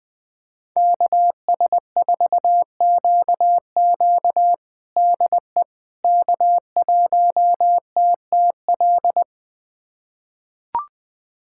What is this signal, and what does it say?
Morse code, heard as a keyed tone, about 20 wpm, 700 Hz. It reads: KS4QQ DE K1TTL
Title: USA Callsign DE Callsign - 5 Characters - 20wpm